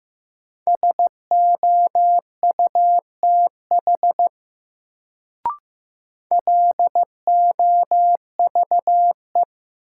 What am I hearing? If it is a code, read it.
Morse code, heard as a keyed tone, about 15 wpm, 700 Hz. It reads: SOUTH LOVE
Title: Sentences from Top 400 Words - New Word Drill - 15wpm